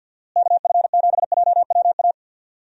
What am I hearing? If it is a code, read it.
Morse code, heard as a keyed tone, about 50 wpm, 700 Hz. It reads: KF7JWA
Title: USA Call Signs - 6 Characters - Repeated 2x - 50wpm